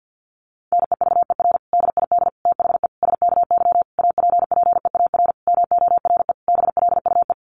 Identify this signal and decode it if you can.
Morse code, heard as a keyed tone, about 50 wpm, 700 Hz. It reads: NEVER DID THE SKY APPEAR MORE BLUE
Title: Sentences from Top 1200 Words 50wpm